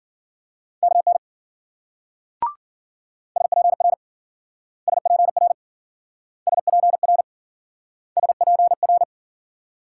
Morse code, heard as a keyed tone, about 54 wpm, 700 Hz. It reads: KN SPR SPR SPR SPR